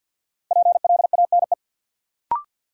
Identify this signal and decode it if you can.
Morse code, heard as a keyed tone, about 48 wpm, 700 Hz. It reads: PLANE
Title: Sentences from Top 1000 Words - Review Part 3 - New Word Drill - 50wpm